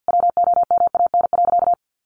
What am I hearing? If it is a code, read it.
Morse code, heard as a keyed tone, about 50 wpm, 700 Hz. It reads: WOMAN.
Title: Sentences from Top 1400 Words 50wpm